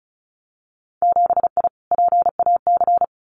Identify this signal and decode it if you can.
Morse code, heard as a keyed tone, about 36 wpm, 700 Hz. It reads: 7I PAC
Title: Winter Field Day Exchanges Encore 35wpm